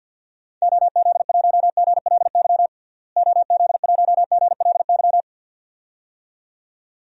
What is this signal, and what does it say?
Morse code, heard as a keyed tone, about 52 wpm, 700 Hz. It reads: OZ1GLY OZ1GLY
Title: DXCC-100 Call Signs - 6 Characters - Repeated 2x - 50wpm